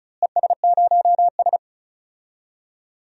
Morse code, heard as a keyed tone, about 36 wpm, 700 Hz. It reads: ES0S
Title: DXCC-100 Call Signs - 4 Characters - Repeated 3x - 35wpm